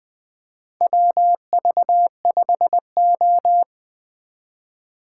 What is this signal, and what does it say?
Morse code, heard as a keyed tone, about 20 wpm, 700 Hz. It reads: WV5O
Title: USA Call Signs - 4 Characters - 2x Character Spacing - 20wpm